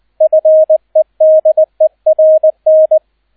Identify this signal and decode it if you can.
Morse code, heard as a keyed tone, about 20 wpm, 630 Hz, with radio noise. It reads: FEDERN